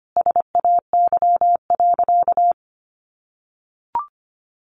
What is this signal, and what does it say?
Morse code, heard as a keyed tone, about 25 wpm, 700 Hz. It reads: SAY.
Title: Sentences from Top 600 Words 25wpm